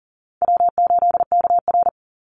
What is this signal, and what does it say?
Morse code, heard as a keyed tone, about 40 wpm, 700 Hz. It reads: W8KR
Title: Call Signs - 40wpm